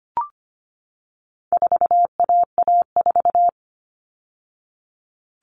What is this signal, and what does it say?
Morse code, heard as a keyed tone, about 25 wpm, 700 Hz. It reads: 4AA4
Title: All Cut Numbers - 4-Digit 25wpm